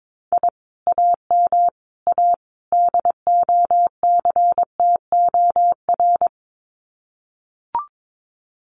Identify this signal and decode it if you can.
Morse code, heard as a keyed tone, about 22 wpm, 700 Hz. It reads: I AM A DOCTOR
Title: Common QSO Phrases Encore 22wpm